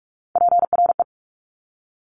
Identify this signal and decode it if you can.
Morse code, heard as a keyed tone, about 46 wpm, 690 Hz. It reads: PRE
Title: Top Three Letter Combinations in Words 45wpm